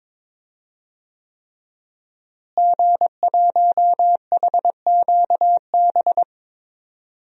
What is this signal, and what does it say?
Morse code, heard as a keyed tone, about 22 wpm, 700 Hz. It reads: G1HQB
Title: DXCC-100 Call Signs - 5 Characters - Repeated 3x - 22wpm